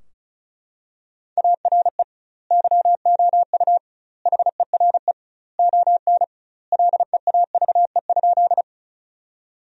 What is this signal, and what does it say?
Morse code, heard as a keyed tone, about 36 wpm, 700 Hz. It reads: ARE YOU HERE ON LEAVE?
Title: 2x Word-Spacing - Sentences from Top 400 Words - 35wpm